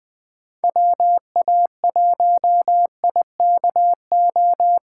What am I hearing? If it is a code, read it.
Morse code, heard as a keyed tone, about 20 wpm, 700 Hz. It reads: WA1IKO